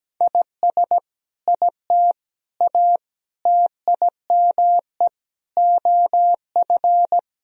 Morse code, heard as a keyed tone, about 17 wpm, 700 Hz. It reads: IS IT A TIME OF
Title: Sentences from Top 1000 Words - Review Part 2 - 17wpm